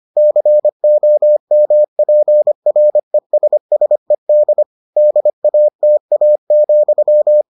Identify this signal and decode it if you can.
Morse code, heard as a keyed tone, about 25 wpm, 600 Hz. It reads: COMPRESSED DATA,